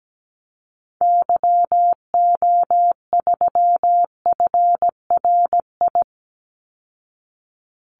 Morse code, heard as a keyed tone, about 17 wpm, 700 Hz. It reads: YO3FRI
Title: Call Signs Repeated 2x - 17wpm